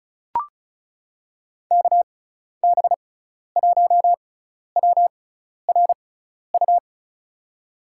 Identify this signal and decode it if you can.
Morse code, heard as a keyed tone, about 36 wpm, spaced as for 12 wpm, 700 Hz. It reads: KB1WRU
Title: USA Call Signs - 6 Characters - 6x Character Spacing - 35wpm